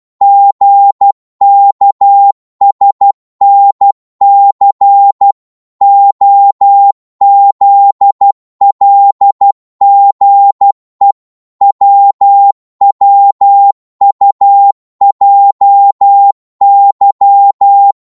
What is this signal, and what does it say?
Morse code, heard as a keyed tone, about 12 wpm, 800 Hz. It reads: GKSNC OZLGE WWUJY